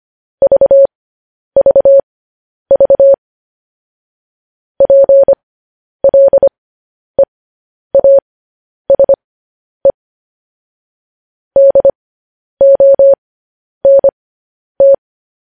Morse code, heard as a keyed tone, about 25 wpm, spaced as for 10 wpm, 550 Hz. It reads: VVV PLEASE DONT
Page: news headlines in morse code at 10 wpm